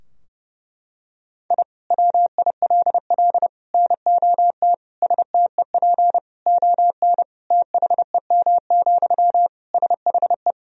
Morse code, heard as a keyed tone, about 31 wpm, 700 Hz. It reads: I WILL NOT STEP ON THEM, SHE
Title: Sentences from Top 1000 Words - Review Part 2 - 30wpm